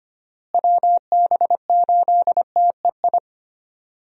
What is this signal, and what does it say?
Morse code, heard as a keyed tone, about 25 wpm, 700 Hz. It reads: WB8TEI
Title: USA Call Signs - 6 Characters - Repeated 2x - 25wpm